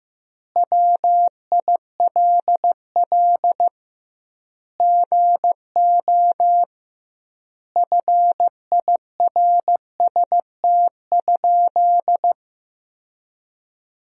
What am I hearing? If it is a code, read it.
Morse code, heard as a keyed tone, about 16 wpm, 700 Hz. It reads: WILL GO FIRST?
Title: 2x Word-Spacing - Sentences from Top 100 Words - 15wpm